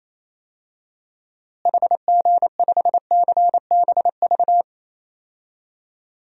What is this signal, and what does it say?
Morse code, heard as a keyed tone, about 28 wpm, 700 Hz. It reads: HG5CBV
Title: DXCC-100 Call Signs - 6 Characters - Repeated 2x - 28wpm